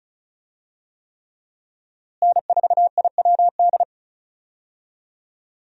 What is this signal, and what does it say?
Morse code, heard as a keyed tone, about 34 wpm, 700 Hz. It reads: N4IWD
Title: USA Call Signs - 5 Characters - Repeated 3x - 35wpm